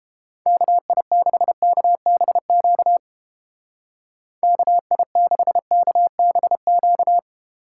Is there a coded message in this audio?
Yes: Morse code from 0.5 to 7.2 s, heard as a keyed tone, about 29 wpm, 700 Hz. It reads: KI6KBQ KI6KBQ